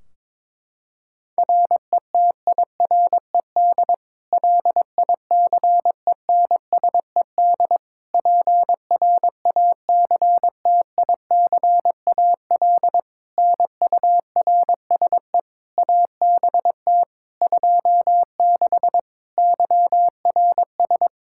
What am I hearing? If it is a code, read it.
Morse code, heard as a keyed tone, about 22 wpm, 700 Hz. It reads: RETIRED LICENSED PRACTICAL NURSE ABT 26 YRS
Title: Practice QSO - Exchange 3 - More Info - 22wpm